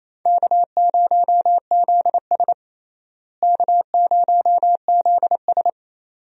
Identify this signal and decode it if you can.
Morse code, heard as a keyed tone, about 28 wpm, 700 Hz. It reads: K0ZS K0ZS
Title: USA Call Signs - 4 Characters - Repeated 2x - 28wpm